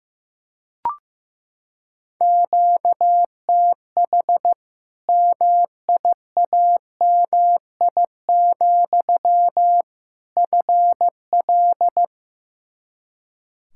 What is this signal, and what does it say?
Morse code, heard as a keyed tone, about 15 wpm, 700 Hz. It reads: QTH MIAMI, FL
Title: Practice QSO - Exchange 1 - Beginning - 15wpm